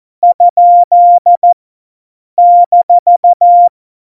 Morse code, heard as a keyed tone, about 14 wpm, 700 Hz. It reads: ? -
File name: Morsekurs / Witze